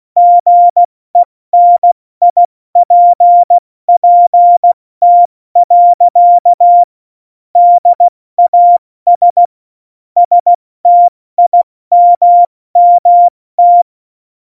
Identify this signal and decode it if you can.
Morse code, heard as a keyed tone, about 16 wpm, 700 Hz. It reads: GENIPPT. DAS STIMMT
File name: Morsekurs / Witze / Witz 22. Oktober 2019, 70 F80